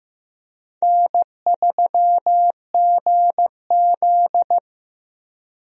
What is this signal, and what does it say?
Morse code, heard as a keyed tone, about 15 wpm, 700 Hz. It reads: N3GZ
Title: USA Call Signs - 4 Characters - Repeated 3x - 15wpm